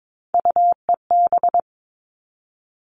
Morse code, heard as a keyed tone, about 22 wpm, 700 Hz. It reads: UEB
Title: All Cut Numbers - 3-Digit 22wpm